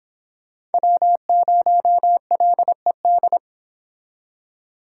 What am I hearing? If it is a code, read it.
Morse code, heard as a keyed tone, about 31 wpm, 700 Hz. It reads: W0LED